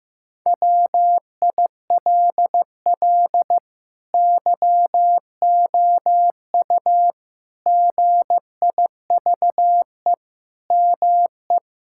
Morse code, heard as a keyed tone, about 15 wpm, 700 Hz. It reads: WILL YOU GIVE ME
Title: Sentences from Top 1600 Words 15wpm